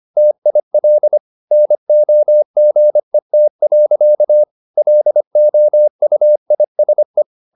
Morse code, heard as a keyed tone, about 25 wpm, 600 Hz. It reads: TIL NOGET. LOUISE
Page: Ferie 25 wpm